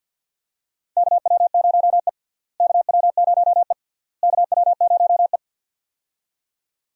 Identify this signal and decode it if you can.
Morse code, heard as a keyed tone, about 47 wpm, 700 Hz. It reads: KW0E KW0E KW0E
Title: USA Call Signs - 4 Characters - Repeated 3x - 50wpm